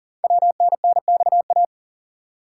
Morse code, heard as a keyed tone, about 43 wpm, 690 Hz. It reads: WNNXA
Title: Code Groups - 5 Letters - 40wpm